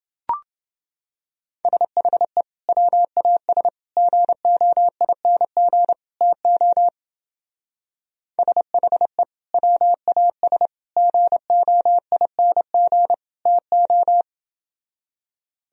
Morse code, heard as a keyed tone, about 28 wpm, 700 Hz. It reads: SHE WAS GOING TO SHE WAS GOING TO